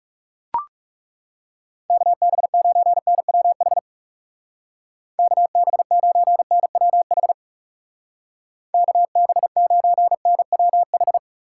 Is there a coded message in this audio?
Yes: Morse code from 1.9 to 11.2 s, heard as a keyed tone, about 40 wpm, 700 Hz. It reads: KB9NWH KB9NWH KB9NWH